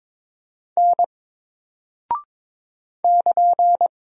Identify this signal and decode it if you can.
Morse code, heard as a keyed tone, about 22 wpm, 700 Hz. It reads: N <KN>
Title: Single Character (Letter:Number:Punctuation:Prosign) 22wpm